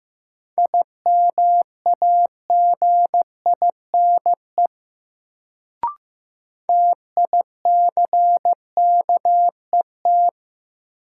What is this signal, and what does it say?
Morse code, heard as a keyed tone, about 16 wpm, 700 Hz. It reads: IMAGINE TICKET
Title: Sentences from Top 1400 Words - New Word Drill - 15wpm